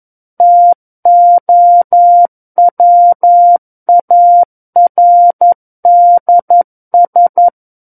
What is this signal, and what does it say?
Morse code, heard as a keyed tone, about 11 wpm, 700 Hz. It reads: TOWARDS
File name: Book One / 02-Chapter 2 - 11 WPM